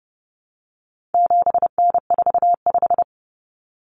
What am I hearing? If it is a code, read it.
Morse code, heard as a keyed tone, about 30 wpm, 700 Hz. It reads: 7N45
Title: Common Cut Numbers - 4-Digit 30wpm